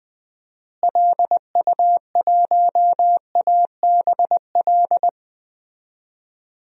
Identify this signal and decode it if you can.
Morse code, heard as a keyed tone, about 19 wpm, 700 Hz. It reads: LU1ABL
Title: DXCC-100 Call Signs - 6 Characters - Repeated 3x - 20wpm